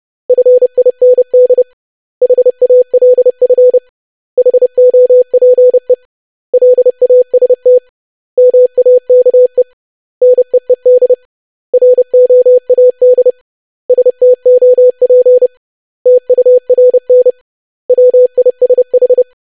Morse code, heard as a keyed tone, about 30 wpm, 500 Hz. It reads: FIND HALF HOPE LAST MAKE NEED ROAD STOP TURN WISH
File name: lesson / web / basic500-25wpm